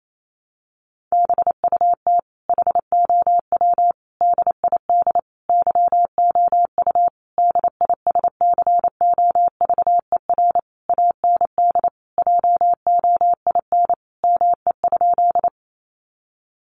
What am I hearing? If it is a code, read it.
Morse code, heard as a keyed tone, about 28 wpm, 700 Hz. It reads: BUT HOW DID YOU DISCOVER AND JOIN ME?
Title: Sentences from Top 1400 Words 28wpm